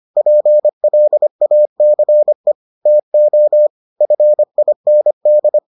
Morse code, heard as a keyed tone, about 25 wpm, 600 Hz. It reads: PLACE TO FIND